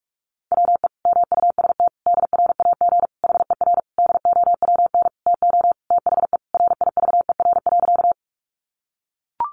Morse code, heard as a keyed tone, about 44 wpm, 700 Hz. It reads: WE MUST DRAG HER DOWN TO THE RIVER.
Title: Sentences from Top 1500 Words 45wpm